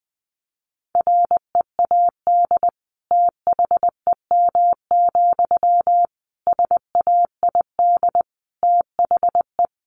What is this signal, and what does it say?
Morse code, heard as a keyed tone, about 20 wpm, 700 Hz. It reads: READ THEM, SAID THE
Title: Sentences from Top 400 Words 20wpm